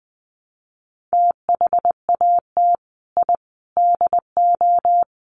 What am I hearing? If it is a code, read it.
Morse code, heard as a keyed tone, about 20 wpm, 700 Hz. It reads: THAT I DO
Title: Sets of 5 Words Encore 20wpm